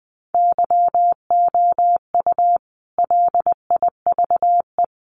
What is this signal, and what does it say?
Morse code, heard as a keyed tone, about 20 wpm, 700 Hz. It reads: YOU LIVE
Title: Sentences from Top 1000 Words 20wpm